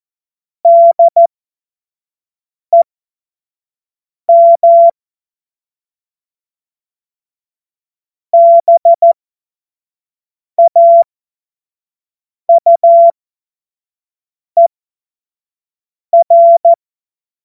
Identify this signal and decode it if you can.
Morse code, heard as a keyed tone, about 14 wpm, spaced as for 5 wpm, 680 Hz. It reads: DEM BAUER
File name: Morsekurs / Witze / Witz 07. September 2021, 25 F70